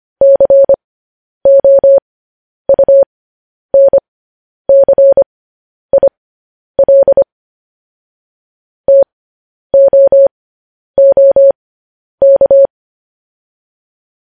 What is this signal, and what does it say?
Morse code, heard as a keyed tone, about 25 wpm, spaced as for 10 wpm, 550 Hz. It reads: COUNCIL TOOK
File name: podcast / News headlines-1x-25x10